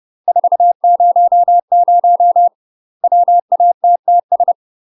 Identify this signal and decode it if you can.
Morse code, heard as a keyed tone, about 30 wpm, 700 Hz. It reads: 400 WATTS
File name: UK MASTER QSOs TRAINING / 30WPM / UK MASTER QSOs 0005